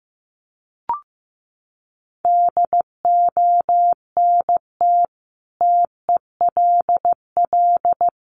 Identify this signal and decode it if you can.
Morse code, heard as a keyed tone, about 15 wpm, 700 Hz. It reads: DONT TELL
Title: Sentences from Top 700 Words 15wpm